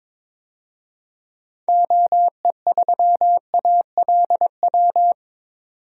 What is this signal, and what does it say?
Morse code, heard as a keyed tone, about 21 wpm, 700 Hz. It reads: OE3ALW
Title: DXCC-100 Call Signs - 6 Characters - 22wpm